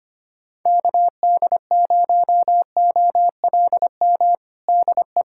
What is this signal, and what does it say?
Morse code, heard as a keyed tone, about 25 wpm, 700 Hz. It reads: KD0OLM DE
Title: USA Callsign DE Callsign - 6 Characters - 25wpm